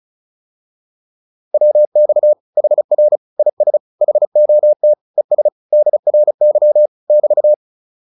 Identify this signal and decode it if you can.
Morse code, heard as a keyed tone, about 37 wpm, 600 Hz. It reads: WX HR IS HOT ES DRY <BT>